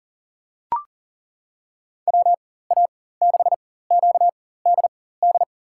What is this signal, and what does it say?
Morse code, heard as a keyed tone, about 40 wpm, spaced as for 19 wpm, 700 Hz. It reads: WA6QDD
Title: USA Call Signs - 6 Characters - 4x Character Spacing - 40wpm